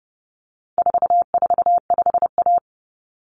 Morse code, heard as a keyed tone, about 28 wpm, 700 Hz. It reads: 445A
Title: Common Cut Numbers - 4-Digit 30wpm